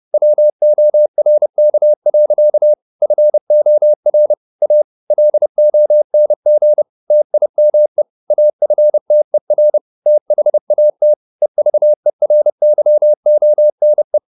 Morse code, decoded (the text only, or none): WORK. FOR A LONG TIME AFTER THAT EVERYONE